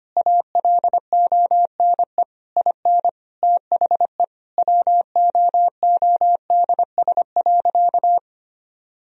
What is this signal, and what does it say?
Morse code, heard as a keyed tone, about 25 wpm, 700 Hz. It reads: ALONE IN THE WOODS.
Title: Sentences from Top 1000 Words - Review Part 2 - 25wpm